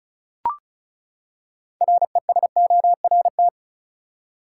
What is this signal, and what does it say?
Morse code, heard as a keyed tone, about 36 wpm, 700 Hz. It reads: RESORT
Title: Six Letter Words - 35wpm